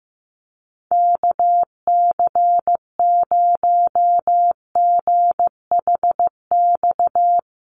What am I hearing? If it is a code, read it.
Morse code, heard as a keyed tone, about 15 wpm, 700 Hz. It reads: KC0GHX
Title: USA Call Signs Repeated 3x 15wpm